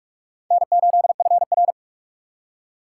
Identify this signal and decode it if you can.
Morse code, heard as a keyed tone, about 46 wpm, 700 Hz. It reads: N8FR
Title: USA Call Signs - 4 Characters - 2x Character Spacing - 45wpm